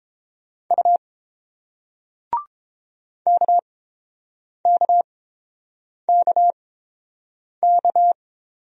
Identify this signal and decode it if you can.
Morse code, heard as a keyed tone, about 30 wpm, spaced as for 7 wpm, 700 Hz. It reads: U KKKK